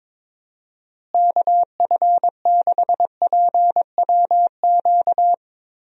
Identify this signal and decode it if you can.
Morse code, heard as a keyed tone, about 22 wpm, 700 Hz. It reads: KF6PWQ
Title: USA Call Signs - 6 Characters - Repeated 3x - 22wpm